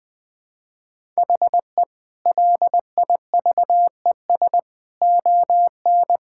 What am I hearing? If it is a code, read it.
Morse code, heard as a keyed tone, about 20 wpm, 700 Hz. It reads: HE LIVES ON
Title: Sentences with 6-Word Phrases - Spoken Beginning - 20wpm